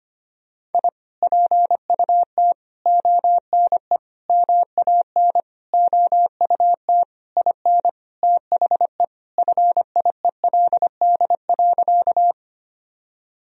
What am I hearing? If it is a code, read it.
Morse code, heard as a keyed tone, about 25 wpm, 700 Hz. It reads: I PUT ONE MAN OUT IN THE FIELD.
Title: Sentences from Top 1000 Words - Review Part 2 - 25wpm